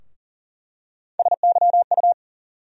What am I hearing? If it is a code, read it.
Morse code, heard as a keyed tone, about 40 wpm, 700 Hz. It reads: SYU
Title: USA Call Sign Postfixes - 3 Letters 40wpm